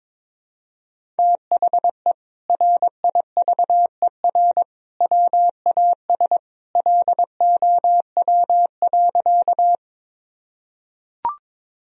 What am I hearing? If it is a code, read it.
Morse code, heard as a keyed tone, about 22 wpm, 700 Hz. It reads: THE RIVER WAS LOW.
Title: Sentences from Top 1000 Words - Review Part 1 - 22wpm